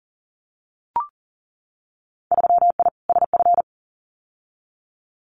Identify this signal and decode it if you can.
Morse code, heard as a keyed tone, about 38 wpm, 700 Hz. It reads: 3I SF
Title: Winter Field Day Exchanges 40wpm